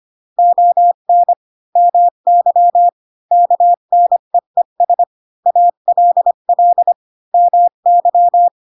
Morse code, heard as a keyed tone, about 25 wpm, 700 Hz. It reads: ON MY KNEES ALL MY
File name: Pinocchio / Pinocchio ch7 20 25 ik1hge